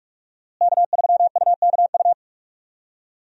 Morse code, heard as a keyed tone, about 46 wpm, 700 Hz. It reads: K3UKU
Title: USA Call Signs - 5 Characters - Repeated 3x - 45wpm